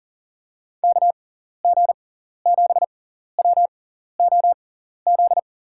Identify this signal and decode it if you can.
Morse code, heard as a keyed tone, about 39 wpm, spaced as for 14 wpm, 700 Hz. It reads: KG7WOZ
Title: USA Call Signs - 6 Characters - 6x Character Spacing - 40wpm